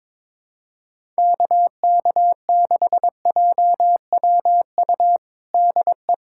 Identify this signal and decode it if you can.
Morse code, heard as a keyed tone, about 22 wpm, 700 Hz. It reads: KK6JWU DE
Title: USA Callsign DE Callsign 22wpm